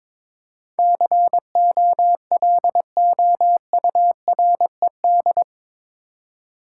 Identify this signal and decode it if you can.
Morse code, heard as a keyed tone, about 21 wpm, 700 Hz. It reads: COLOURED
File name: practice / 22wpm / Eight Letter Words - 22wpm